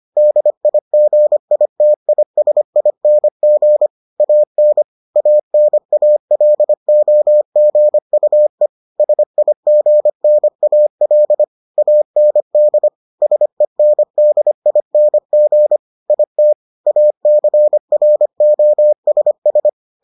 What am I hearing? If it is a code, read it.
Morse code, heard as a keyed tone, about 25 wpm, 600 Hz. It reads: DIGITISING AN ANALOGUE SIGNAL AND SENDING IT ACROSS